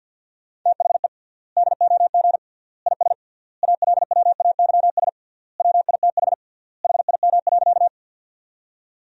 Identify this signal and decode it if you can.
Morse code, heard as a keyed tone, about 50 wpm, 700 Hz. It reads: THE DOG IS ALWAYS WITH HIM.
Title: 3x Word-Spacing - Sentences from Top 400 Words - 50wpm